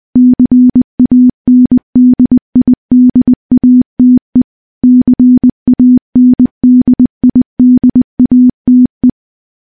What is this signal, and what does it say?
Morse code, heard as a keyed tone, about 20 wpm, 250 Hz. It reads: CANDIDATE CANDIDATE
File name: morse-training-20WPM